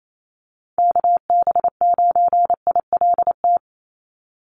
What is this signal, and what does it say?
Morse code, heard as a keyed tone, about 28 wpm, 700 Hz. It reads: KB9ILT
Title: NA SOTA Chaser Callsigns 28wpm